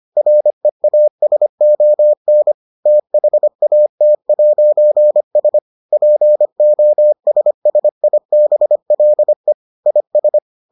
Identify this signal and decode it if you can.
Morse code, heard as a keyed tone, about 25 wpm, 600 Hz. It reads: REASON THAT'S POSSIBLE IS